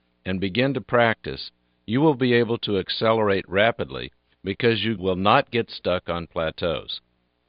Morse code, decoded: none